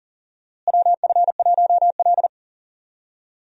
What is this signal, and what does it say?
Morse code, heard as a keyed tone, about 40 wpm, 700 Hz. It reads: WF1L
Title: USA Call Signs - 4 Characters - Repeated 3x - 40wpm